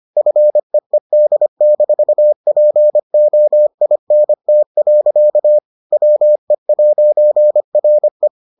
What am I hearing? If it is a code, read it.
Morse code, heard as a keyed tone, about 25 wpm, 600 Hz. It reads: FEED-POINT. WE'RE